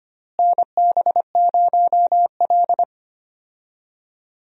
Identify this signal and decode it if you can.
Morse code, heard as a keyed tone, about 25 wpm, 700 Hz. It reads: NB0L
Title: USA Call Signs - 4 Characters - Repeated 2x - 25wpm